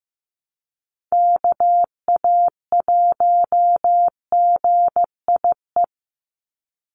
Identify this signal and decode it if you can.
Morse code, heard as a keyed tone, about 15 wpm, 700 Hz. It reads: KA1GIE
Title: USA Call Signs 15wpm